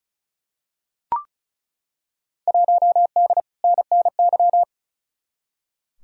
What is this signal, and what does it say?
Morse code, heard as a keyed tone, about 36 wpm, 700 Hz. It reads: 1D NNY
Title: Field Day Exchanges - 35wpm